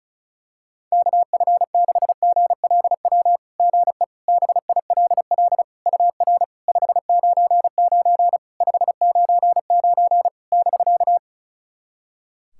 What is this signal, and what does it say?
Morse code, heard as a keyed tone, about 36 wpm, 700 Hz. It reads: KF6GLW GE BILL UR 599 599 <BK>
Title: SOTA QSO Practice 35wpm